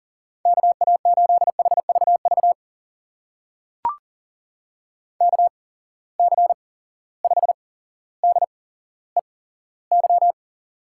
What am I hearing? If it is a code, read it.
Morse code, heard as a keyed tone, about 39 wpm, spaced as for 17 wpm, 700 Hz. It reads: KA8HVV KC5DEY
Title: USA Call Signs - 8x Character Spacing - 40wpm